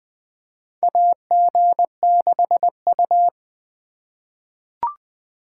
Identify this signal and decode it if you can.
Morse code, heard as a keyed tone, about 20 wpm, 700 Hz. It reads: AG6U
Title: USA Call Signs - 4 Characters - 3x Character Spacing - 20wpm